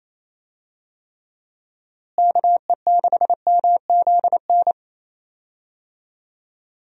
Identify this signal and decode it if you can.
Morse code, heard as a keyed tone, about 28 wpm, 700 Hz. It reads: KE6MZN
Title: USA Call Signs - 6 Characters - Repeated 3x - 28wpm